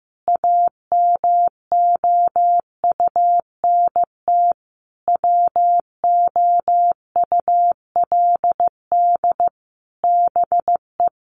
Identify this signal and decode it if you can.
Morse code, heard as a keyed tone, about 15 wpm, 700 Hz. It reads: AMOUNT WOULD BE
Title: Sentences from Top 1400 Words 15wpm